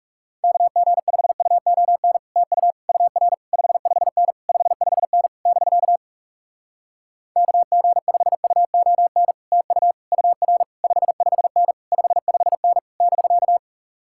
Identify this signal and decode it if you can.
Morse code, heard as a keyed tone, about 38 wpm, 700 Hz. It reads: KG5UON TU UR 55N 55N <BK> KG5UON TU UR 55N 55N <BK>